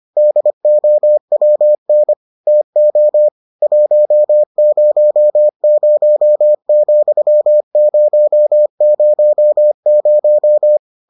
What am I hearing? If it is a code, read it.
Morse code, heard as a keyed tone, about 25 wpm, 600 Hz. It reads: DOWN TO 100,000